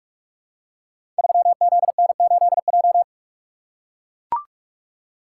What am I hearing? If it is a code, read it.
Morse code, heard as a keyed tone, about 44 wpm, 700 Hz. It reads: 3ZN8J
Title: Code Groups - 5 Letter:Numbers - 45wpm